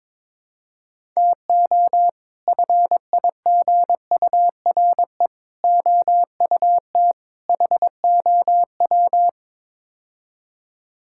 Morse code, heard as a keyed tone, about 21 wpm, 700 Hz. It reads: TO FIGURE OUT HOW
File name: practice / 22wpm / Build A Phrase - 5-Word Phrases - 22wpm